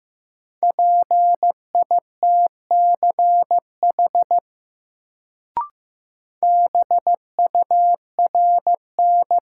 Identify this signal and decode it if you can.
Morse code, heard as a keyed tone, about 15 wpm, 700 Hz. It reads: PITCH BURN
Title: Sentences from Top 1300 Words - New Word Drill - 15wpm